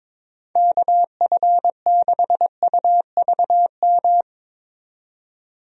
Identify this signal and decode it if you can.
Morse code, heard as a keyed tone, about 22 wpm, 700 Hz. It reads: KF6UVM
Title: USA Call Signs - 6 Characters - 22wpm